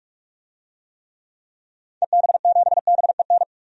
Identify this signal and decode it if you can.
Morse code, heard as a keyed tone, about 46 wpm, 700 Hz. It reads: EB7BEN
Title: DXCC-100 Call Signs - 6 Characters - 45wpm